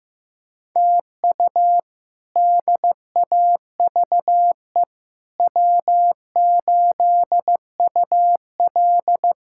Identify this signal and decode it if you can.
Morse code, heard as a keyed tone, about 15 wpm, 700 Hz. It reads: TU DAVE W8UL
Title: K1USN SST QSO Practice - 15wpm